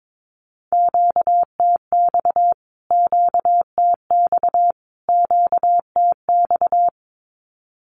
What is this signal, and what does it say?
Morse code, heard as a keyed tone, about 22 wpm, 700 Hz. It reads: QTX QTX QTX
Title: Q-Codes Repeated 3x 22wpm